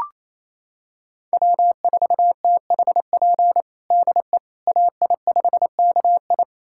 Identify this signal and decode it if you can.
Morse code, heard as a keyed tone, about 28 wpm, 700 Hz. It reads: W4THP DE AI5KI
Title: USA Callsign DE Callsign - 5 Characters - 28wpm